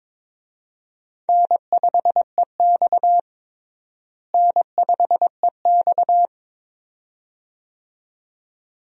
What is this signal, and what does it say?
Morse code, heard as a keyed tone, about 22 wpm, 700 Hz. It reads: N5EX N5EX
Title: USA Call Signs - 4 Characters - Repeated 2x - 22wpm